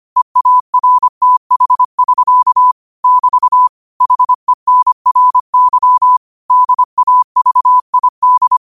Morse code, heard as a keyed tone, about 25 wpm, 1000 Hz. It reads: EARTH<SK> <BT> HENRY DAVID
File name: qotdfiles / 25wpm